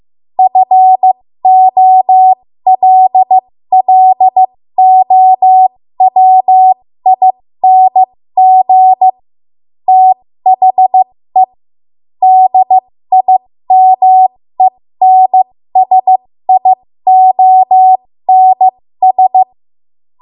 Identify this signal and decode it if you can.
Morse code, heard as a keyed tone, about 15 wpm, spaced as for 13 wpm, 750 Hz. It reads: FOLLOWING THE DIMENSIONS
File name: Morse / 13WPM